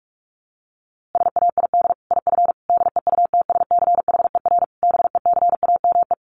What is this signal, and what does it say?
Morse code, heard as a keyed tone, about 44 wpm, 700 Hz. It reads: SAID IF DEUTSCHER BECAME
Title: A Sound Of Thunder 45wpm